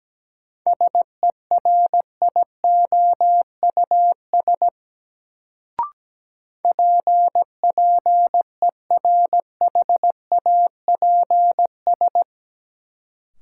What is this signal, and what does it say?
Morse code, heard as a keyed tone, about 17 wpm, 700 Hz. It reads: SERIOUS PPERHAPS
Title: Sentences from Top 1100 Words - New Word Drill - 17wpm